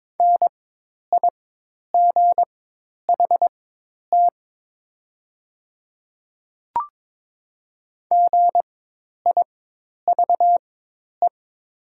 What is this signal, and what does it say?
Morse code, heard as a keyed tone, about 22 wpm, spaced as for 10 wpm, 700 Hz. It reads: NIGHT GIVE
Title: 4x Character Spacing - Top 300 Words - 22wpm